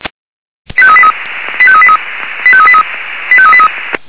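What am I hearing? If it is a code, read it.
Five-tone selective call (ZVEI), heard as quick runs of five tones, tones of about 70 ms. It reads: 85383 85383 85383 85383
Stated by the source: ZVEI five-tone selective call received by SDR.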